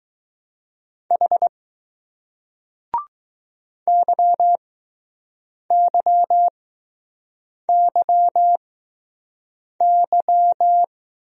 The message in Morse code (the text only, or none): H YYYY